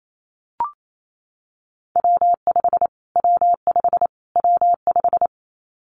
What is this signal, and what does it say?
Morse code, heard as a keyed tone, about 28 wpm, 700 Hz. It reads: W5 W5 W5
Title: Callsign Prefixes for Top 25 DXCC - Repeated 3x - 28wpm